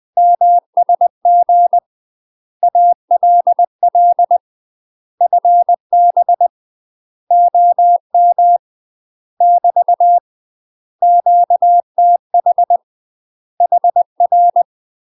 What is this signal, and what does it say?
Morse code, heard as a keyed tone, about 20 wpm, 700 Hz. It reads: MSG ALL FB OM <BT> QTH HR
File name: UK MASTER QSOs TRAINING / 20WPM / UK MASTER QSOs 0021